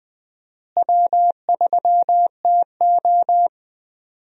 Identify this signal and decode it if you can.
Morse code, heard as a keyed tone, about 20 wpm, 700 Hz. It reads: W3TO
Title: USA Call Signs - 4 Characters - Repeated 3x - 20wpm